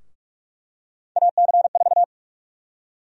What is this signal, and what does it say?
Morse code, heard as a keyed tone, about 45 wpm, 700 Hz. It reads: AC4
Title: USA Call Sign Prefixes - Repeated 2x - 45wpm